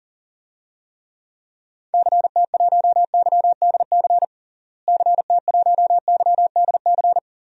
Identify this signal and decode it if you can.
Morse code, heard as a keyed tone, about 37 wpm, 700 Hz. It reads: CT1YDC CT1YDC
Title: DXCC-100 Call Signs - 6 Characters - Repeated 3x - 40wpm